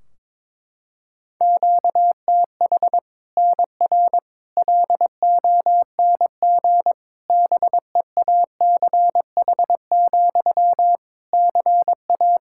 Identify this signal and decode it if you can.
Morse code, heard as a keyed tone, about 21 wpm, 700 Hz. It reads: QTH NR LONG BEACH, CA
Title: K1USN SST QSO Practice 22wpm